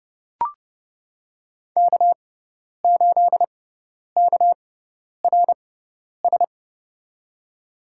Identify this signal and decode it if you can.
Morse code, heard as a keyed tone, about 32 wpm, spaced as for 10 wpm, 700 Hz. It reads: K8KRS
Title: USA Call Signs - 5 Characters - 6x Character Spacing - 30wpm